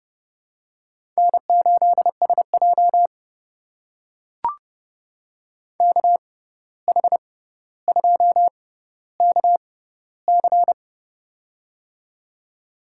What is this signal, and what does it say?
Morse code, heard as a keyed tone, about 30 wpm, spaced as for 14 wpm, 700 Hz. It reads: N8SJ KH2KC
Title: USA Call Signs - 6x Character Spacing - 30wpm